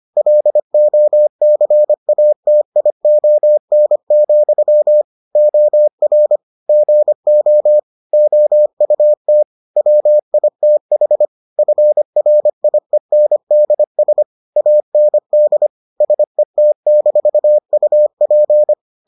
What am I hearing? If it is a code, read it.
Morse code, heard as a keyed tone, about 25 wpm, 600 Hz. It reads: LOCATION, OR GO OUT WITH FRIENDS AND SET-UP